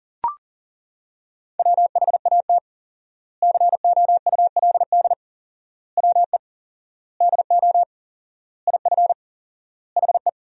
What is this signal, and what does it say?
Morse code, heard as a keyed tone, about 40 wpm, 700 Hz. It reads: WHAT COULD WE DO IF HE
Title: 4x Word-Spacing - Sentences from Top 400 Words - 40wpm